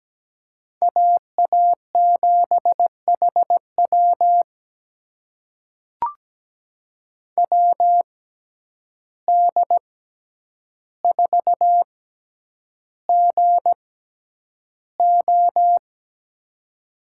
Morse code, heard as a keyed tone, about 17 wpm, spaced as for 9 wpm, 700 Hz. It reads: AA7HW WD4GO
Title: USA Call Signs - 6x Character Spacing - 17wpm